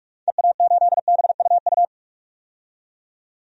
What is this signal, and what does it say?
Morse code, heard as a keyed tone, about 47 wpm, 690 Hz. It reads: EA8BUU
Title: DXCC-100 Call Signs - 6 Characters - 45wpm